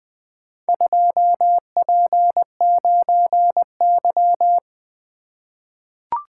Morse code, heard as a keyed tone, about 20 wpm, 700 Hz. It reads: 2P9Y
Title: Four Letter-Number Combinations - 20wpm